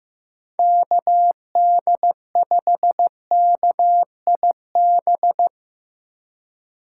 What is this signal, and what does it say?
Morse code, heard as a keyed tone, about 15 wpm, 700 Hz. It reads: KD5KIB
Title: USA Call Signs - 6 Characters - 3x Character Spacing - 15wpm